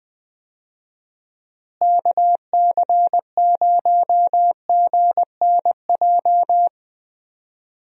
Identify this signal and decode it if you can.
Morse code, heard as a keyed tone, about 20 wpm, 700 Hz. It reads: KC0GNJ
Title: USA Call Signs - 6 Characters - Repeated 3x - 20wpm